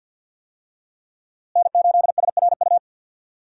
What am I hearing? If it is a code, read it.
Morse code, heard as a keyed tone, about 51 wpm, 700 Hz. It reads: N8SRU
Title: USA Call Signs - 5 Characters - Repeated 2x - 50wpm